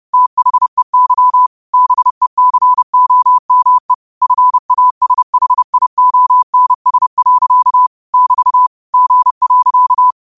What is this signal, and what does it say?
Morse code, heard as a keyed tone, about 30 wpm, 1000 Hz. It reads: THEY BECOME FASHIONS. <BT> G.
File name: qotdfiles / 30wpm